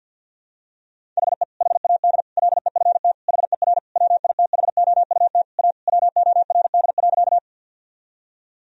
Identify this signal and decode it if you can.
Morse code, heard as a keyed tone, about 52 wpm, 700 Hz. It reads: HE HAD LEFT HER WITHOUT A WORD.
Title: Sentences from Top 1000 Words - Review Part 3 - 50wpm